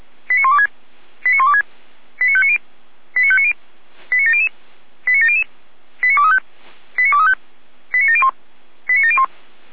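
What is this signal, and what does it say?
Five-tone selective call (ZVEI), heard as quick runs of five tones, tones of about 70 ms. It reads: 89126 89126 89590 89590 89700 89700 89235 89235 89701 89701
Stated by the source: ZVEI five-tone selective call received by SDR.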